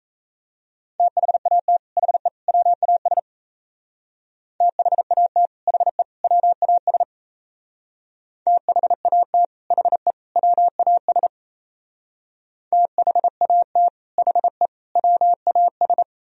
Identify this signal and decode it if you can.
Morse code, heard as a keyed tone, about 35 wpm, 700 Hz. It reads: THAT HE WAS THAT HE WAS THAT HE WAS THAT HE WAS